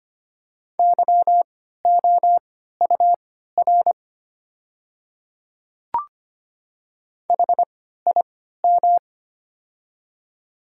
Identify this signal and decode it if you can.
Morse code, heard as a keyed tone, about 25 wpm, spaced as for 14 wpm, 690 Hz. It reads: YOUR HIM
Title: 3x Character Spacing - Top 100 Words - 25wpm